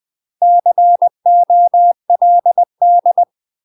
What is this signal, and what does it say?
Morse code, heard as a keyed tone, about 20 wpm, 700 Hz. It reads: COLD
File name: UK MASTER QSOs TRAINING / 20WPM / UK MASTER QSOs 0058